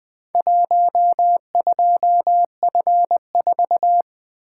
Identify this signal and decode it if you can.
Morse code, heard as a keyed tone, about 20 wpm, 700 Hz. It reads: 12F4
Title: Four Letter-Number Combinations - 20wpm